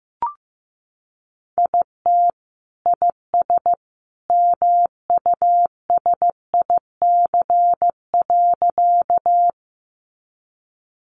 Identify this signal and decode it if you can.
Morse code, heard as a keyed tone, about 15 wpm, 700 Hz. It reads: IT IS MUSIC.
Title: Sentences from Top 300 Words 15wpm